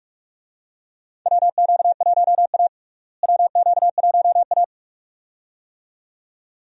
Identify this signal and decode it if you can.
Morse code, heard as a keyed tone, about 47 wpm, 700 Hz. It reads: WQ1A WQ1A
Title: USA Call Signs - 4 Characters - Repeated 2x - 45wpm